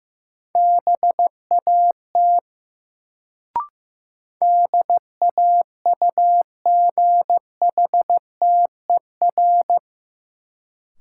Morse code, heard as a keyed tone, about 15 wpm, 700 Hz. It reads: BAT DAUGHTER
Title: Sentences from Top 1500 Words - New Word Drill Alt - 15wpm